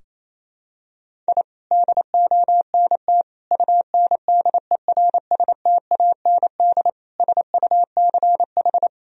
Morse code, heard as a keyed tone, about 28 wpm, 700 Hz. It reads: I DONT UNDERSTAND SUCH
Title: Sentences from Top 1000 Words - Review Part 3 - 28wpm